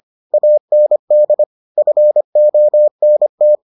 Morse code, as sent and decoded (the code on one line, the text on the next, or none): .- -. -.. / ..-. --- -. -
AND FONT